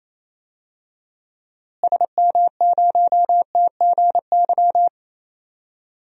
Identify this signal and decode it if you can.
Morse code, heard as a keyed tone, about 28 wpm, 700 Hz. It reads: SM0TGY
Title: DXCC-100 Call Signs - 6 Characters - Repeated 2x - 28wpm